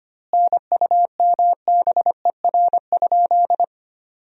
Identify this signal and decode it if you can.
Morse code, heard as a keyed tone, about 25 wpm, 700 Hz. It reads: NUMBER?
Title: 2x Word-Spacing - Sentences from Top 100 Words - 25wpm